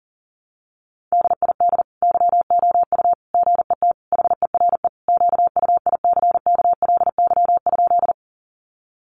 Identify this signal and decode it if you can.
Morse code, heard as a keyed tone, about 40 wpm, 700 Hz. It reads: DID YOU GET HERE QUICKLY?
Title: Sentences from Top 700 Words 40wpm